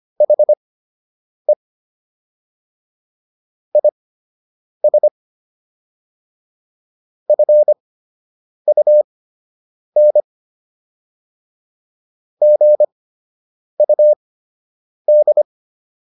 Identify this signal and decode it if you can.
Morse code, heard as a keyed tone, about 25 wpm, spaced as for 8 wpm, 600 Hz. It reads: HE IS FUN GUD